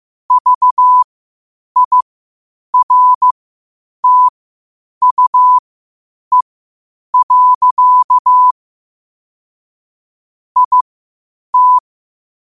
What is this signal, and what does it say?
Morse code, heard as a keyed tone, about 15 wpm, spaced as for 8 wpm, 1000 Hz. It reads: VIRTUE. IT
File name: qotdfiles / 8wpm